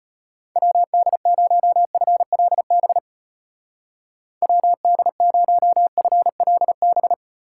Morse code, heard as a keyed tone, about 33 wpm, 700 Hz. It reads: WD0FLB WD0FLB